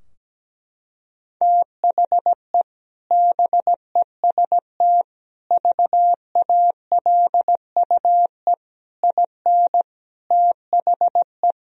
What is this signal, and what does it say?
Morse code, heard as a keyed tone, about 17 wpm, 700 Hz. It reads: THE BEST VALUE IN THE
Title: Sentences from Top 1000 Words - Review Part 3 - 17wpm